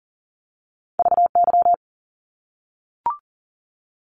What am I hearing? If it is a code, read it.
Morse code, heard as a keyed tone, about 40 wpm, 700 Hz. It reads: VY
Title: Abbreviations 40wpm